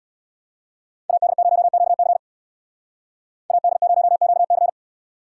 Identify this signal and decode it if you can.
Morse code, heard as a keyed tone, about 77 wpm, 700 Hz. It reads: AD0QC AD0QC